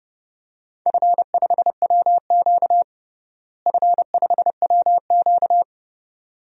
Morse code, heard as a keyed tone, about 30 wpm, 700 Hz. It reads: F5WQ F5WQ
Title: DXCC-100 Call Signs - 4 Characters - Repeated 2x - 30wpm